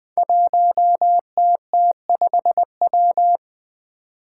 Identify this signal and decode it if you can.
Morse code, heard as a keyed tone, about 20 wpm, 700 Hz. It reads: 1TT5W
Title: Five Challenging Letter-Number Combinations 20wpm